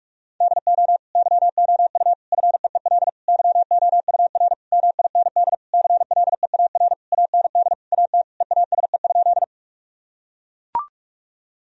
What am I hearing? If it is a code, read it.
Morse code, heard as a keyed tone, about 41 wpm, 700 Hz. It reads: DO YOU FEEL YOUR MIND CLEAR AND AT EASE?
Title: Sentences from Top 1000 Words - Review Part 1 - 45wpm